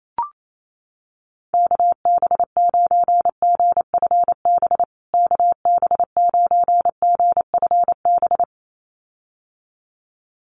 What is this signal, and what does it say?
Morse code, heard as a keyed tone, about 28 wpm, 700 Hz. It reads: KB9GFB KB9GFB
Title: USA Call Signs Repeated 2x 28wpm